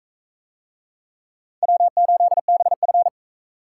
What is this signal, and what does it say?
Morse code, heard as a keyed tone, about 38 wpm, 700 Hz. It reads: W8BF